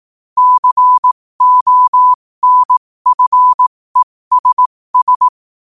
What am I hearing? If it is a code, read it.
Morse code, heard as a keyed tone, about 18 wpm, spaced as for 15 wpm, 1000 Hz. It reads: CONFESS
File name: qotdfiles / 15wpm